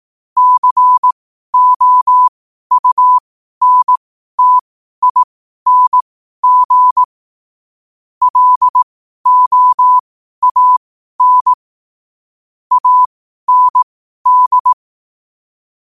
Morse code, heard as a keyed tone, about 18 wpm, spaced as for 12 wpm, 1000 Hz. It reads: COUNTING LOAN AND
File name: qotdfiles / 12wpm